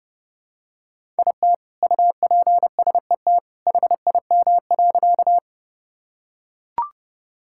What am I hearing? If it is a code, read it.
Morse code, heard as a keyed tone, about 30 wpm, 700 Hz. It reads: IT UPSET HIM.
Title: Sentences from Top 1000 Words - Review Part 5 - 30wpm